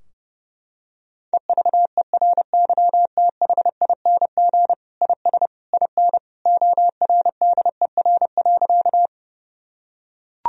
Morse code, heard as a keyed tone, about 32 wpm, 700 Hz. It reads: EVERYTHING IS IN ORDER.
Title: Sentences from Top 1000 Words - Review Part 2 - 30wpm